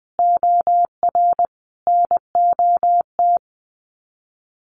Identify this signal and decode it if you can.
Morse code, heard as a keyed tone, about 20 wpm, 700 Hz. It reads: OR NOT
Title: Sets of 5 Words Encore 20wpm